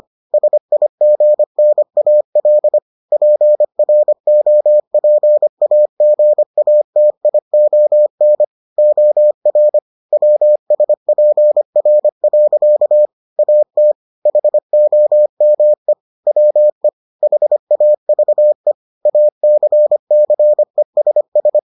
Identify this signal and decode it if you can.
Morse code, heard as a keyed tone, about 25 wpm, 600 Hz. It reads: SIGNAL PROPAGATION OR WSPR. AT HOME WE HAVE ACCESS